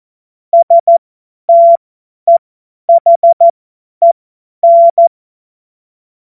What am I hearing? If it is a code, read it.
Morse code, heard as a keyed tone, about 14 wpm, spaced as for 10 wpm, 680 Hz. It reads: STEHEN
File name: Morsekurs / Witze / Witz 16. Februar 2021, 50 F70